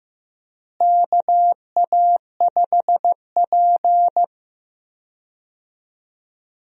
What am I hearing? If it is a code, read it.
Morse code, heard as a keyed tone, about 15 wpm, 700 Hz. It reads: KA5P
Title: USA Call Signs - 4 Characters - Repeated 2x - 15wpm